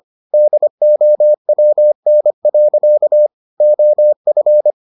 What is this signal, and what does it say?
Morse code, heard as a keyed tone, about 25 wpm, 600 Hz. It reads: DOWN. OF